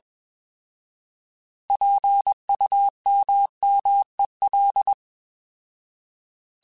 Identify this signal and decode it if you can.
Morse code, heard as a keyed tone, about 21 wpm, 800 Hz. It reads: PUMMEL
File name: Alphabetical / Alphabetical mp3s / Ps @20